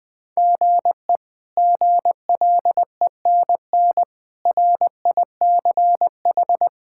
Morse code, heard as a keyed tone, about 20 wpm, 700 Hz. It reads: GE GLENN RICH
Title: K1USN SST QSO Practice - 20wpm